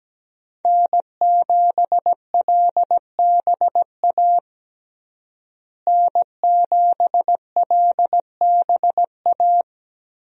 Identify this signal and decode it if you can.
Morse code, heard as a keyed tone, about 17 wpm, 700 Hz. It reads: N7LBA N7LBA
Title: USA Call Signs - 5 Characters - Repeated 2x - 17wpm